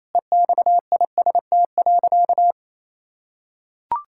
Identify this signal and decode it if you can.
Morse code, heard as a keyed tone, about 28 wpm, 700 Hz. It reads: EXIST.
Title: Sentences from Top 1000 Words - Review Part 5 - 28wpm